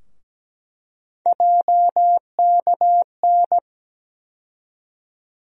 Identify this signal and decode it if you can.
Morse code, heard as a keyed tone, about 17 wpm, 700 Hz. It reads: JKN
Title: USA Call Sign Postfixes - 3 Letters 17wpm